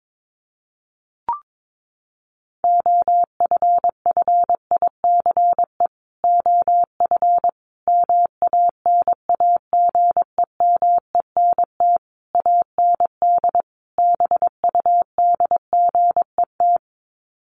Morse code, heard as a keyed tone, about 22 wpm, 700 Hz. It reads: OFFICE OF MANAGEMENT AND BUDGET
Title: Sets of 5 Words Encore2 22wpm